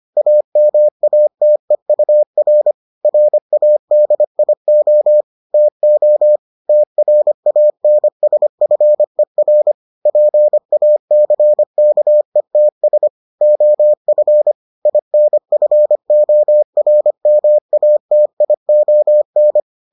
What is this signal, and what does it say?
Morse code, heard as a keyed tone, about 25 wpm, 600 Hz. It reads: AMATEUR RADIO TO TRANSFER PACKETS OF INFORMATION